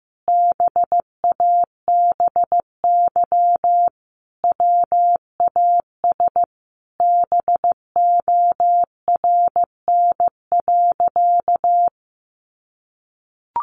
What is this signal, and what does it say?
Morse code, heard as a keyed tone, about 15 wpm, 700 Hz. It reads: BABY WAS BORN.
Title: Sentences from Top 1300 Words 15wpm